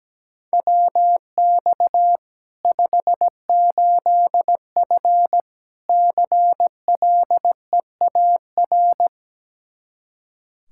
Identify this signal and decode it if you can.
Morse code, heard as a keyed tone, about 17 wpm, 700 Hz. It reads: WX 58F CLEAR
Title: Practice QSO - Exchange 2 - Info - 17wpm